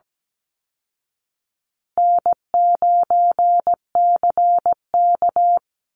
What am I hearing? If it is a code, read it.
Morse code, heard as a keyed tone, about 17 wpm, 700 Hz. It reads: N9CK
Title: Easy Call Signs Repeated 3x - 17wpm